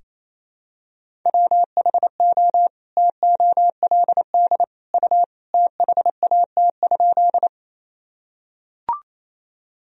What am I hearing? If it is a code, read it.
Morse code, heard as a keyed tone, about 28 wpm, 700 Hz. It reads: WHO TOLD U THAT?
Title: Sentences from Top 400 Words and Abbreviations 28wpm